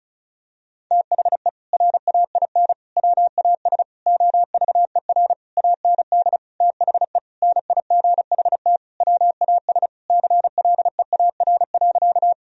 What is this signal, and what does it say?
Morse code, heard as a keyed tone, about 36 wpm, 700 Hz. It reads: THE RAIN WAS OVER AND THE NIGHT WAS CLEAR.
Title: Sentences from Top 1000 Words - Review Part 1 - 35wpm